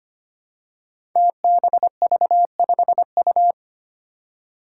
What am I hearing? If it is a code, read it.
Morse code, heard as a keyed tone, about 25 wpm, 700 Hz. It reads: TBV5U
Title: Five Challenging Letter-Number Combinations 25wpm